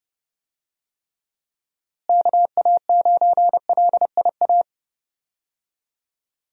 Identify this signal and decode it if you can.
Morse code, heard as a keyed tone, about 30 wpm, 700 Hz. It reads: KA9LIA
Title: USA Call Signs - 6 Characters - Repeated 2x - 30wpm